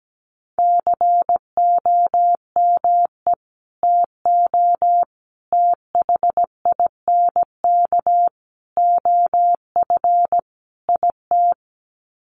Morse code, heard as a keyed tone, about 17 wpm, 700 Hz. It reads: COME TO THINK OF IT
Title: Sets of 5 Words Encore 17wpm